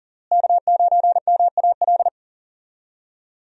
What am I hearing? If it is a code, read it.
Morse code, heard as a keyed tone, about 40 wpm, 700 Hz. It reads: K9MAL
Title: USA Call Signs - 5 Characters - Repeated 3x - 40wpm